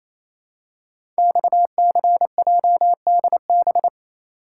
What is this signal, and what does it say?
Morse code, heard as a keyed tone, about 29 wpm, 700 Hz. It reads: XCJDB
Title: Code Groups - 5 Letters - 28wpm